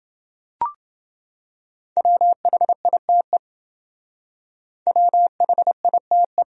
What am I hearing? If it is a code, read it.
Morse code, heard as a keyed tone, about 28 wpm, 700 Hz. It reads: WHITE WHITE